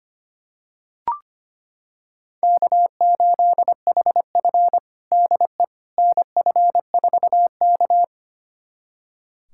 Morse code, heard as a keyed tone, about 25 wpm, 700 Hz. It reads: K8HF DE NF4K
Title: USA Callsign DE Callsign - 4 Characters - 25wpm